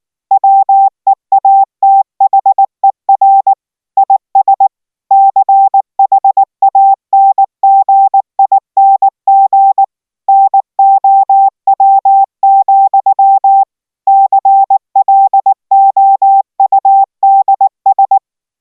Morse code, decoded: WEATHER IS CHANGING NOW, CLOUDS

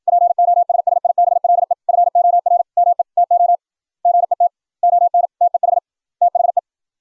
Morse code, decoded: POSSIBLE FOR ME TO GET ON TEH THE

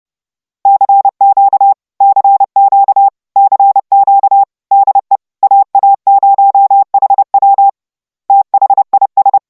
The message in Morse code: CQ CQ CQ DE AA0HW THIS

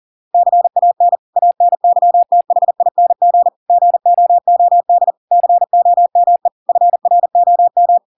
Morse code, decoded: CAN ANYTHING GOOD COME FROM

